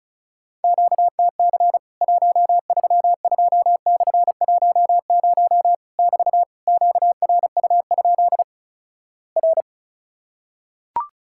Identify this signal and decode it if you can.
Morse code, heard as a keyed tone, about 36 wpm, 700 Hz. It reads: QTC 132/10 <BT> QRU? R
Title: Worked All Europe DX Contest - QTC Exchange - 35wpm